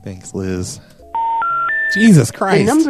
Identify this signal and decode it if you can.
Special information tone, with speech.